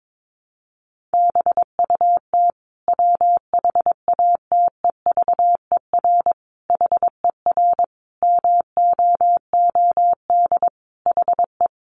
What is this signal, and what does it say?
Morse code, heard as a keyed tone, about 22 wpm, 700 Hz. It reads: BUT WHATEVER HER MOOD HE
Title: Sentences from Top 1100 Words 22wpm